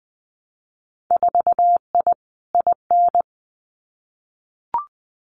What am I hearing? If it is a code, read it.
Morse code, heard as a keyed tone, about 20 wpm, 700 Hz. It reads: 4I IN
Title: Winter Field Day Exchanges 20wpm